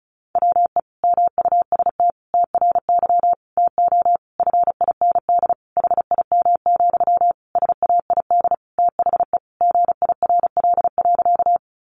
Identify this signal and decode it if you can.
Morse code, heard as a keyed tone, about 34 wpm, 700 Hz. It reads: WE MUST TRY TO FIND HIM, SAID THE GIRL.
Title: Sentences from Top 400 Words 35wpm